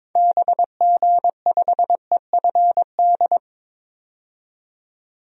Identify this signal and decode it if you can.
Morse code, heard as a keyed tone, about 22 wpm, 700 Hz. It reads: BG5EFD
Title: DXCC-100 Call Signs - 6 Characters - 22wpm